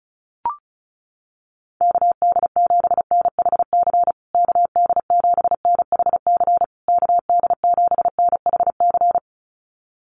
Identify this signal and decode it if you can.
Morse code, heard as a keyed tone, about 34 wpm, 700 Hz. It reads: KD7NHC KD7NHC KD7NHC
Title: USA Call Signs Repeated 3x 35wpm